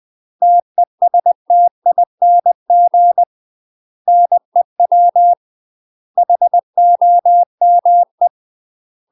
Morse code, heard as a keyed tone, about 20 wpm, 700 Hz. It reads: TESTING NEW HOME
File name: UK MASTER QSOs TRAINING / 20WPM / UK MASTER QSOs 0059